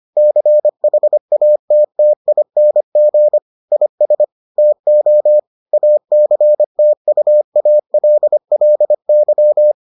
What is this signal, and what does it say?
Morse code, heard as a keyed tone, about 25 wpm, 600 Hz. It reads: CHATTING IS TO ACTUALLY